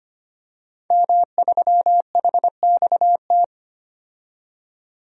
Morse code, heard as a keyed tone, about 25 wpm, 700 Hz. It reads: M3HXT
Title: DXCC-100 Call Signs - 5 Characters - 25wpm